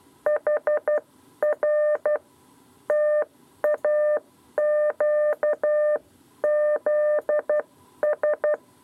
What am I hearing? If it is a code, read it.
Morse code, heard as a keyed tone, about 12 wpm, spaced as for 10 wpm, 600 Hz. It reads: HR TAQZS
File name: 7b lezione cw 2